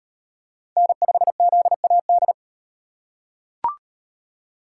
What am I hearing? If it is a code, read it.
Morse code, heard as a keyed tone, about 34 wpm, 700 Hz. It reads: N5ZAD